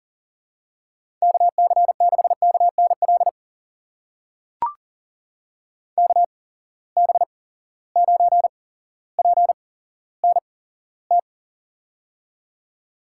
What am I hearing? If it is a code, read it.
Morse code, heard as a keyed tone, about 39 wpm, spaced as for 17 wpm, 700 Hz. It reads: KC6KNL KB9PNT
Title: USA Call Signs - 8x Character Spacing - 40wpm